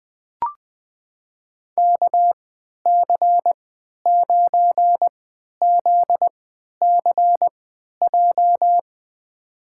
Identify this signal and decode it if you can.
Morse code, heard as a keyed tone, about 20 wpm, spaced as for 11 wpm, 700 Hz. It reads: KC9ZCJ
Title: USA Call Signs - 3x Character Spacing - 20wpm